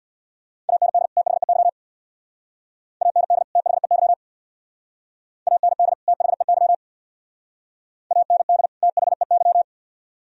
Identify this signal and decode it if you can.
Morse code, heard as a keyed tone, about 63 wpm, 700 Hz. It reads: AND THEY AND THEY AND THEY AND THEY